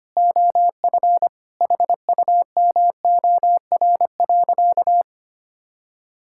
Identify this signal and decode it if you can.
Morse code, heard as a keyed tone, about 25 wpm, 700 Hz. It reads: OF HUMOR.
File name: practice / 25wpm / Sentences with 2-Word Phrases - 25wpm